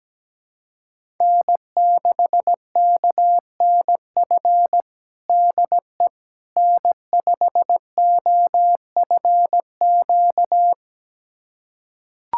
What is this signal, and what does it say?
Morse code, heard as a keyed tone, about 17 wpm, 700 Hz. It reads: N6KNF DE N5OFQ
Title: USA Callsign DE Callsign - 5 Characters - 17wpm